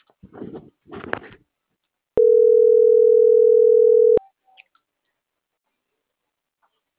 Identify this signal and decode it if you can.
Ringback tone (North America), 440 + 480 Hz.